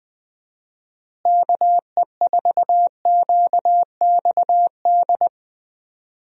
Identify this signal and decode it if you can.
Morse code, heard as a keyed tone, about 21 wpm, 700 Hz. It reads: KE4QXD
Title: USA Call Signs - 6 Characters - Repeated 3x - 20wpm